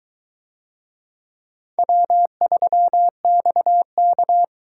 Morse code, heard as a keyed tone, about 22 wpm, 700 Hz. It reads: W3XK